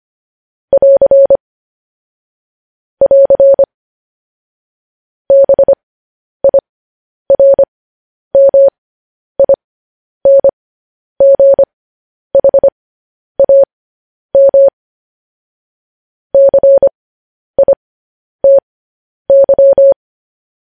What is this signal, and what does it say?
Morse code, heard as a keyed tone, about 25 wpm, spaced as for 10 wpm, 550 Hz. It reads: <AR> <AR> BIRMINGHAM CITY
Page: news-headlines-in-morse-code-at-10-wpm